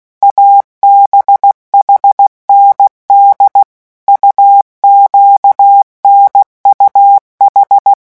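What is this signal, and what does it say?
Morse code, heard as a keyed tone, about 16 wpm, 800 Hz. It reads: ABHND UQNUH
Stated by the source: Morse letter-group revision drill at 15 words per minute.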